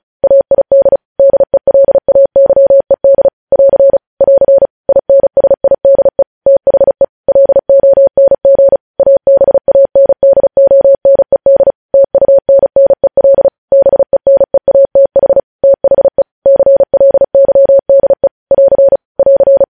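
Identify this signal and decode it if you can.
Morse code, heard as a keyed tone, about 36 wpm, 550 Hz. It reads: AID DELAYED <AR> <AR> INSIDE THE LONG ABANDONED TUNNEL BENEATH THE CLYDE <AR> <AR>
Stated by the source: Morse code news headlines practice sent at 35 wpm.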